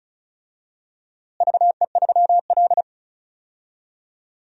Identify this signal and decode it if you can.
Morse code, heard as a keyed tone, about 34 wpm, 700 Hz. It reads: VE3L
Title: DXCC-100 Call Signs - 4 Characters - Repeated 2x - 35wpm